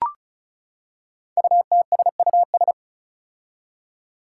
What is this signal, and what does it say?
Morse code, heard as a keyed tone, about 36 wpm, 700 Hz. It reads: UTSUS
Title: Five Challenging Letter-Number Combinations 35wpm